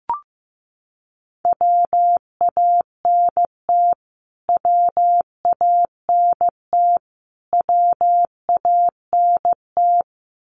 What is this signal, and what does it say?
Morse code, heard as a keyed tone, about 15 wpm, 700 Hz. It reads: WANT WANT WANT
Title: Top 500 Words Repeated 15wpm